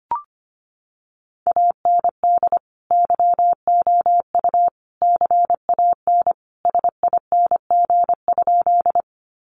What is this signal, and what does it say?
Morse code, heard as a keyed tone, about 25 wpm, 700 Hz. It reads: AND YOU CAN SING?
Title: Sentences from Top 400 Words 25wpm